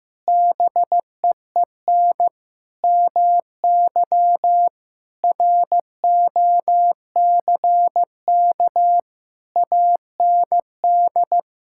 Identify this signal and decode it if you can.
Morse code, heard as a keyed tone, about 15 wpm, 700 Hz. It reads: BEEN MY ROCK AND
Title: Sentences from Top 1000 Words - Review Part 5 - 15wpm